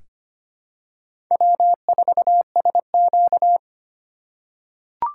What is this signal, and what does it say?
Morse code, heard as a keyed tone, about 25 wpm, 700 Hz. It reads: W4SQ
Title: USA Call Signs - 4 Characters - 3x Character Spacing - 25wpm